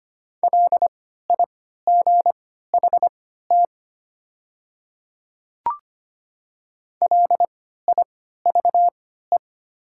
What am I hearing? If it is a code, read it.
Morse code, heard as a keyed tone, about 25 wpm, spaced as for 14 wpm, 700 Hz. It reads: LIGHT LIVE
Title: 3x Character Spacing - Top 200 Words - 25wpm